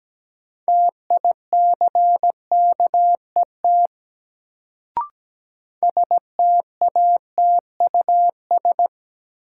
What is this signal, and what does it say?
Morse code, heard as a keyed tone, about 17 wpm, 700 Hz. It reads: TICKET STATUS
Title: Sentences from Top 1400 Words - New Word Drill - 17wpm